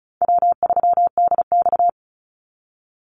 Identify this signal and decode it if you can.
Morse code, heard as a keyed tone, about 36 wpm, 700 Hz. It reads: W3DX
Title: NA SOTA Activator Callsigns 35wpm